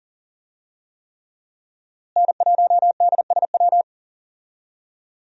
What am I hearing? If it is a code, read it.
Morse code, heard as a keyed tone, about 40 wpm, 700 Hz. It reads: N1DSW
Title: USA Call Signs - 5 Characters - Repeated 3x - 40wpm